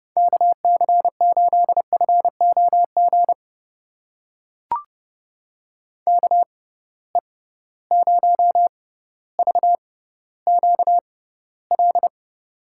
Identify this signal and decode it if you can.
Morse code, heard as a keyed tone, about 31 wpm, spaced as for 15 wpm, 700 Hz. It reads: KC8FOG KE0VQL
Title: USA Call Signs - 6 Characters - 6x Character Spacing - 30wpm